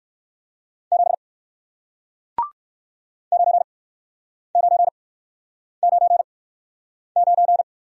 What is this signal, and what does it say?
Morse code, heard as a keyed tone, about 56 wpm, spaced as for 9 wpm, 700 Hz. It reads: 7 9999
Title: Speed-Racing Single Letter-Number 45wpm